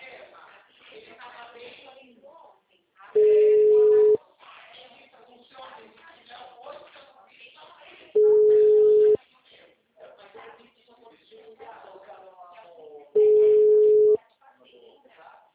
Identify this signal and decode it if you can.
Ringback tone (Europe, 425 Hz), with speech.